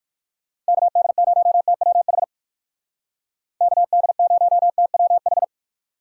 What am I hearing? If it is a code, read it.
Morse code, heard as a keyed tone, about 55 wpm, 700 Hz. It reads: KD0TWH KD0TWH
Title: Speed Racing - USA 2x3 Call Signs 45wpm